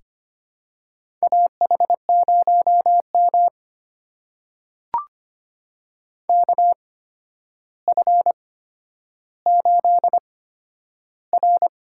Morse code, heard as a keyed tone, about 25 wpm, spaced as for 11 wpm, 700 Hz. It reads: AH0M KF8R
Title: USA Call Signs - 4 Characters - 8x Character Spacing - 25wpm